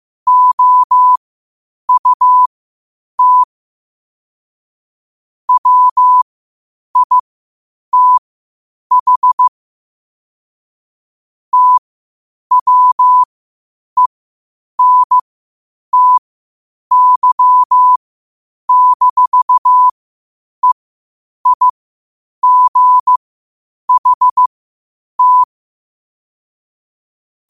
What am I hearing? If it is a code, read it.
Morse code, heard as a keyed tone, about 15 wpm, spaced as for 8 wpm, 1000 Hz. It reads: OUT WITH TWENTY-EIGHT
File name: qotdfiles / 8wpm